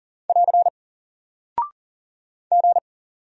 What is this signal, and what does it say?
Morse code, heard as a keyed tone, about 40 wpm, 700 Hz. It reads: <AR> G
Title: Single Character (Letter:Number:Punctuation:Prosign) Alt 40wpm